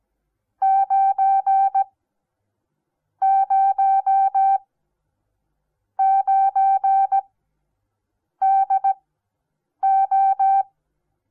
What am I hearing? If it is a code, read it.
Morse code, heard as a keyed tone, about 17 wpm, spaced as for 6 wpm, 780 Hz. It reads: 909DO
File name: Onlinelehrgang / morsen / DJ4UF Morsen08